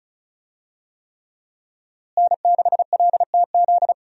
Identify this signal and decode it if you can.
Morse code, heard as a keyed tone, about 36 wpm, 700 Hz. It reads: N6LTZ
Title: USA Call Signs - 5 Characters - Repeated 3x - 35wpm